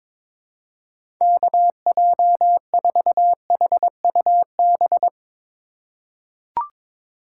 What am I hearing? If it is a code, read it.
Morse code, heard as a keyed tone, about 22 wpm, 700 Hz. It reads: KJ4HUB
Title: USA Call Signs - 4x Character Spacing - 22wpm